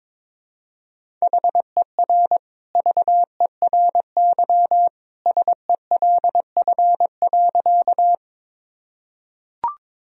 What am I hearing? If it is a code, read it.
Morse code, heard as a keyed tone, about 22 wpm, 700 Hz. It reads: HER VERY SELF.
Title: Sentences from Top 1000 Words - Review Part 1 - 22wpm